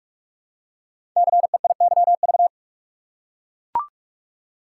Morse code, heard as a keyed tone, about 44 wpm, 700 Hz. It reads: CEIYV
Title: Code Groups - 5 Letter:Numbers - 45wpm